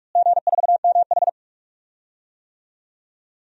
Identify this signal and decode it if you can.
Morse code, heard as a keyed tone, about 46 wpm, 700 Hz. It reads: G4MH
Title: DXCC-100 Call Signs - 4 Characters - Repeated 3x - 45wpm